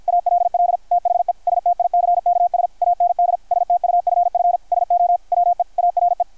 Morse code, heard as a keyed tone, about 51 wpm, 720 Hz, with radio noise. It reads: ALL THE STICKS AND STUFF SO WE ARE
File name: archived / CRSnet